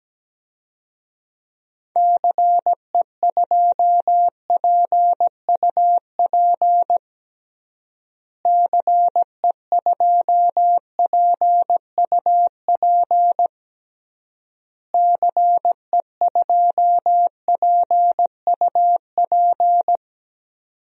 Morse code, heard as a keyed tone, about 17 wpm, 700 Hz. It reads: CE2PUP CE2PUP CE2PUP
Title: DXCC-100 Call Signs - 6 Characters - Repeated 3x - 17wpm